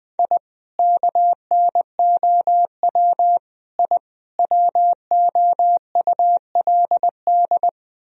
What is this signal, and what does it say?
Morse code, heard as a keyed tone, about 21 wpm, 700 Hz. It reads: I KNOW I WOULD
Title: Sentences from Top 1000 Words - Review Part 3 - 20wpm